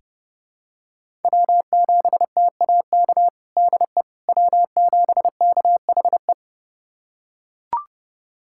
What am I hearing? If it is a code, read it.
Morse code, heard as a keyed tone, about 30 wpm, 700 Hz. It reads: W7TAK DE W7KHE
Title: USA Callsign DE Callsign - 5 Characters - 30wpm